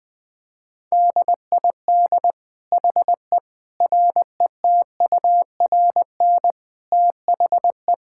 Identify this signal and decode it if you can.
Morse code, heard as a keyed tone, about 20 wpm, 700 Hz. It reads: DID HE RETURN THE
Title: Sentences from Top 1000 Words - Review Part 5 - 20wpm